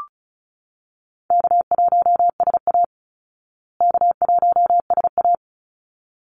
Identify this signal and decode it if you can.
Morse code, heard as a keyed tone, about 34 wpm, 700 Hz. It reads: K1SA K1SA
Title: Easy Call Signs Repeated 2x - 35wpm